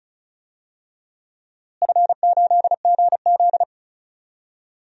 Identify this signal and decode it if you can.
Morse code, heard as a keyed tone, about 34 wpm, 700 Hz. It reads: F8GZ
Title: DXCC-100 Call Signs - 4 Characters - Repeated 2x - 35wpm